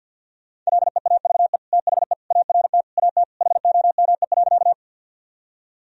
Morse code, heard as a keyed tone, about 50 wpm, 700 Hz. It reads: LEAVE THE ART AT HOME.
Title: Sentences from Top 1000 Words - Review Part 3 - 50wpm